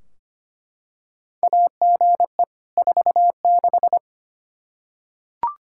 Morse code, heard as a keyed tone, about 25 wpm, 700 Hz. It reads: AGE 46
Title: Practice QSO - Exchange 3 - More Info - 25wpm